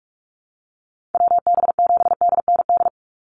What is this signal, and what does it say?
Morse code, heard as a keyed tone, about 44 wpm, 700 Hz. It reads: WB7DND
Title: Call Signs Repeated 2x - 45wpm